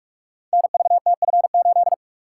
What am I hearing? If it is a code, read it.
Morse code, heard as a keyed tone, about 44 wpm, 700 Hz. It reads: NVTF8
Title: Code Groups - 5 Letter:Numbers - 45wpm